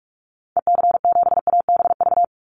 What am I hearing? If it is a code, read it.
Morse code, heard as a keyed tone, about 44 wpm, 700 Hz. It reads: EC7ABV
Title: Call Signs - 45wpm